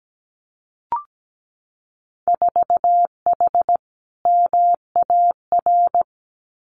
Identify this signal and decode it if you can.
Morse code, heard as a keyed tone, about 17 wpm, 700 Hz. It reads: 4H MAR
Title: Winter Field Day Exchanges Encore 17wpm